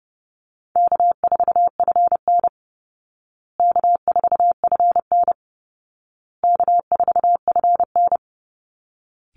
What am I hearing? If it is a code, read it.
Morse code, heard as a keyed tone, about 30 wpm, 700 Hz. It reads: K4FN K4FN K4FN
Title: Easy Call Signs Repeated 3x - 30wpm